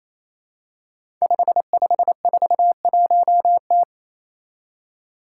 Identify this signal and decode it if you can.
Morse code, heard as a keyed tone, about 27 wpm, 700 Hz. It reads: 5541T
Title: Five Challenging Letter-Number Combinations 28wpm